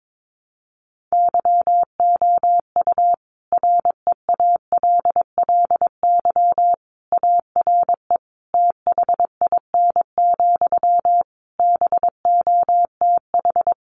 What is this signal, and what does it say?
Morse code, heard as a keyed tone, about 22 wpm, 700 Hz. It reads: YOU REALLY ARE THIN, BOTH
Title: Sentences from Top 1200 Words 22wpm